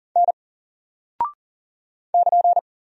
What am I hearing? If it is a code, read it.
Morse code, heard as a keyed tone, about 40 wpm, 700 Hz. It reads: N <KN>
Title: Single Character (Letter:Number:Punctuation:Prosign) 40wpm